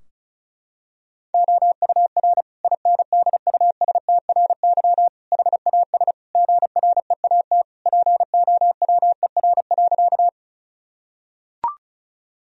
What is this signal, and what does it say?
Morse code, heard as a keyed tone, about 34 wpm, 700 Hz. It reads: OUR INDUSTRY HAS GREAT POWER.
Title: Sentences from Top 1000 Words - Review Part 4 - 35wpm